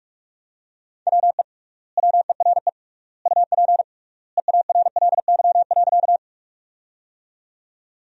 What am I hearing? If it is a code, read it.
Morse code, heard as a keyed tone, about 46 wpm, 700 Hz. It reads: WE WERE UP EARLY.
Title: 3x Word-Spacing - Sentences from Top 400 Words - 45wpm